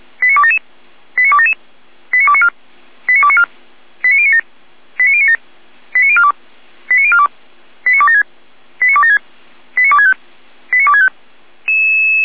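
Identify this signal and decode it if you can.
Five-tone selective call (ZVEI), heard as quick runs of five tones, tones of about 70 ms. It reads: 89277 89277 89294 89294 89097 89097 89042 89042 89276 89276 89265 89265
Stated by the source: ZVEI five-tone selective call received on SDR.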